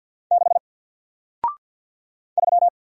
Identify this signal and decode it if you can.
Morse code, heard as a keyed tone, about 50 wpm, 700 Hz. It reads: 6 3
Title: Single Character (Letter:Number:Punctuation:Prosign) 50wpm